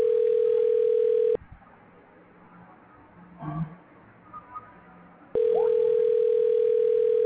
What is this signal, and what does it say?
Ringback tone (North America), 440 + 480 Hz.
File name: recordings / REf6cfa a5d5a75cd5a7977f2a8